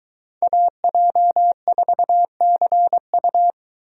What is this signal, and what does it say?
Morse code, heard as a keyed tone, about 23 wpm, 700 Hz. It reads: AJ4CU